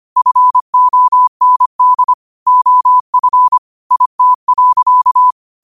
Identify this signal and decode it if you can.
Morse code, heard as a keyed tone, about 25 wpm, 1000 Hz. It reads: FOND OF IT.
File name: qotdfiles / 25wpm